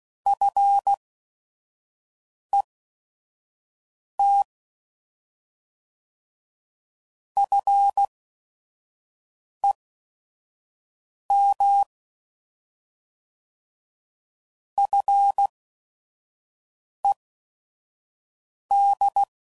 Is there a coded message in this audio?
Yes: Morse code from 0.3 to 19.2 s, heard as a keyed tone, about 16 wpm, spaced as for 5 wpm, 800 Hz. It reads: FET FEM FED